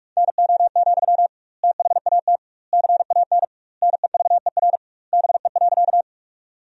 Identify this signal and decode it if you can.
Morse code, heard as a keyed tone, about 41 wpm, 700 Hz. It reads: NO, THAT CAN NEVER BE.
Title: 2x Word-Spacing - Sentences from Top 300 Words - 45wpm